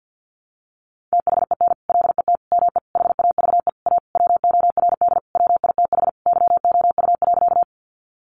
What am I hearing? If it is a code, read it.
Morse code, heard as a keyed tone, about 52 wpm, 700 Hz. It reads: THEN LET ME HAVE A WORD WITH YOU.
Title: Sentences from Top 300 Words 50wpm